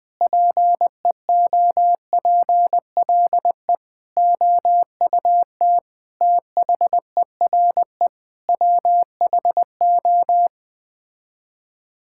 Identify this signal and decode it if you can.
Morse code, heard as a keyed tone, about 20 wpm, 700 Hz. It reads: PEOPLE OUT THERE WHO
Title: Sets of 9 Words - 20wpm